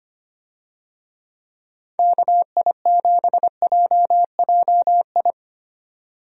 Morse code, heard as a keyed tone, about 25 wpm, 700 Hz. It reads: KI7JJI
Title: USA Call Signs - 6 Characters - 25wpm